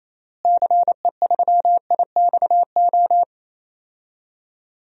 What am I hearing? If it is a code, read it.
Morse code, heard as a keyed tone, about 28 wpm, 700 Hz. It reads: CE3IXO
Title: DXCC-100 Call Signs - 6 Characters - Repeated 2x - 28wpm